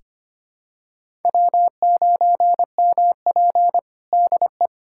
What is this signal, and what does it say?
Morse code, heard as a keyed tone, about 25 wpm, 700 Hz. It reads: W9MP DE
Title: USA Callsign DE Callsign - 4 Characters - 25wpm